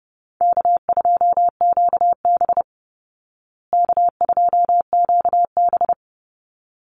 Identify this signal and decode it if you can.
Morse code, heard as a keyed tone, about 28 wpm, 700 Hz. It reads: K2QB K2QB
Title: Call Signs Repeated 3x - 30wpm